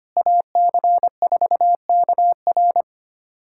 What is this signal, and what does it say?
Morse code, heard as a keyed tone, about 25 wpm, 700 Hz. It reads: AC4KR
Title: USA Callsign DE Callsign 25wpm